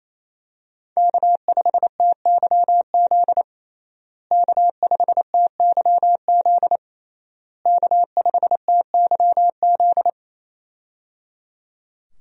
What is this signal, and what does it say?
Morse code, heard as a keyed tone, about 27 wpm, 700 Hz. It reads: K5TYZ K5TYZ K5TYZ
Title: USA Call Signs - 5 Characters - Repeated 3x - 28wpm